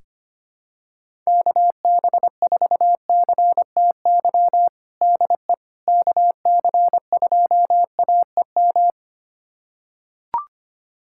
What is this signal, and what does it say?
Morse code, heard as a keyed tone, about 25 wpm, 700 Hz. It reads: KB4CTY DE KC2AEM
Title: USA Callsign DE Callsign 25wpm